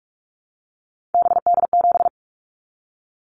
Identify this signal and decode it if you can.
Morse code, heard as a keyed tone, about 44 wpm, 700 Hz. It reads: BD7
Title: Callsign Prefixes for Top 100 DXCC - 45wpm